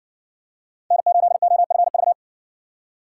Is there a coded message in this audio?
Yes: Morse code from 0.9 to 2.1 s, heard as a keyed tone, about 60 wpm, 700 Hz. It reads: N8OFV